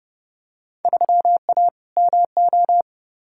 Morse code, heard as a keyed tone, about 30 wpm, 700 Hz. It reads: 3A MO
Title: Field Day Exchanges Encore - 30wpm